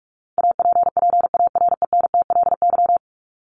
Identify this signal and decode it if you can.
Morse code, heard as a keyed tone, about 44 wpm, 700 Hz. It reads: APPARENTLY
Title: LY Suffix - Top 100 Words with Base Words 45wpm